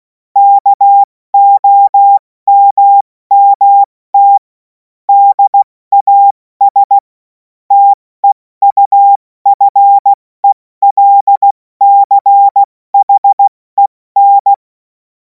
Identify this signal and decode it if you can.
Morse code, heard as a keyed tone, about 16 wpm, spaced as for 14 wpm, 800 Hz. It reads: KOMMT DAS TEUFELCHEN
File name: Morsekurs / Witze / Witz 03. März 2026 70 F80